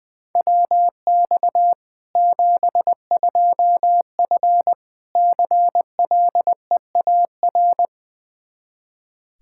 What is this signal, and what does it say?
Morse code, heard as a keyed tone, about 21 wpm, 700 Hz. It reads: WX 72F CLEAR
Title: Practice QSO - Exchange 2 - Info - 20wpm